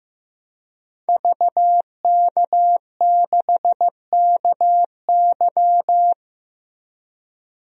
Morse code, heard as a keyed tone, about 15 wpm, 700 Hz. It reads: VK6KY
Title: DXCC-100 Call Signs - 5 Characters - Repeated 2x - 15wpm